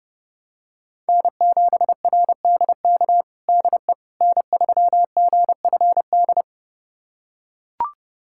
Morse code, heard as a keyed tone, about 30 wpm, 700 Hz. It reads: N7RDK DE N3GFD
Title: USA Callsign DE Callsign - 5 Characters - 30wpm